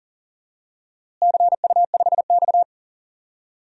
Morse code, heard as a keyed tone, about 40 wpm, 700 Hz. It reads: CU5X
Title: DXCC-100 Call Signs - 4 Characters - Repeated 2x - 40wpm